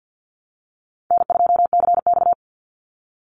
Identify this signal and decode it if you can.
Morse code, heard as a keyed tone, about 48 wpm, 700 Hz. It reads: N2CX
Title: Call Signs Repeated 3x - 50wpm